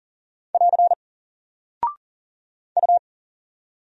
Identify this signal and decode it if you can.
Morse code, heard as a keyed tone, about 39 wpm, 700 Hz. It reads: <AR> U
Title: Single Character (Letter:Number:Punctuation:Prosign) 40wpm